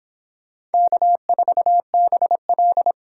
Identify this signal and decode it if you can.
Morse code, heard as a keyed tone, about 24 wpm, 700 Hz. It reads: K4BL